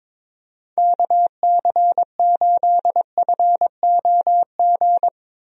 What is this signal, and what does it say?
Morse code, heard as a keyed tone, about 22 wpm, 700 Hz. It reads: KC8FOG
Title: USA Call Signs - 6 Characters - 22wpm